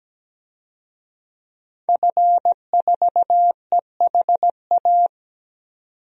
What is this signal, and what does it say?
Morse code, heard as a keyed tone, about 17 wpm, 700 Hz. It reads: F4EHA
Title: DXCC-100 Call Signs - 5 Characters - 17wpm